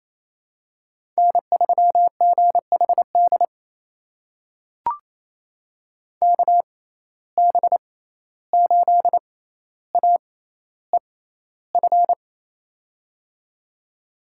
Morse code, heard as a keyed tone, about 28 wpm, spaced as for 14 wpm, 700 Hz. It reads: N3GHD KB8AEF
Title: USA Call Signs - 6x Character Spacing - 28wpm